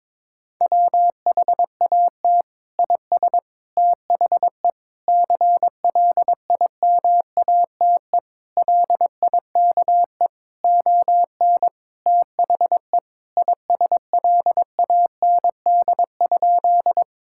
Morse code, heard as a keyed tone, about 22 wpm, 700 Hz. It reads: WHAT IS THE CLIMATE LIKE ON THE ISLAND?
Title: Sentences from Top 1600 Words 22wpm